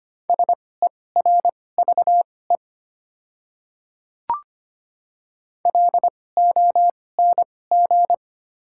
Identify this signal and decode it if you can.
Morse code, heard as a keyed tone, about 25 wpm, spaced as for 18 wpm, 700 Hz. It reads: SERVE LONG
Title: 2x Character Spacing - Top 500 Words - 25wpm